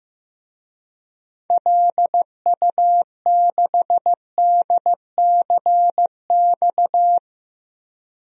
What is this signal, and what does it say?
Morse code, heard as a keyed tone, about 15 wpm, 700 Hz. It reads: LU6DCX
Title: DXCC-100 Call Signs - 6 Characters - 15wpm